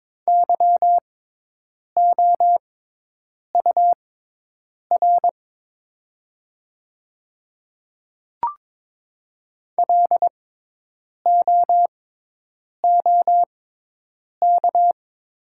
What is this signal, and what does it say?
Morse code, heard as a keyed tone, about 22 wpm, spaced as for 8 wpm, 690 Hz. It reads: YOUR LOOK
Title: 6x Character Spacing - Top 100 Words - 22wpm